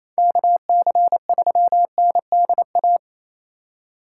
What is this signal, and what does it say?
Morse code, heard as a keyed tone, about 34 wpm, 700 Hz. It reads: KC3NDA
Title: Speed Racing - USA 2x3 Call Signs 28wpm